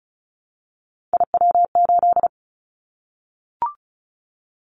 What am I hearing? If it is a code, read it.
Morse code, heard as a keyed tone, about 34 wpm, 700 Hz. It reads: IW8
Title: Callsign Prefixes for Top 25 DXCC - Repeated 3x - 35wpm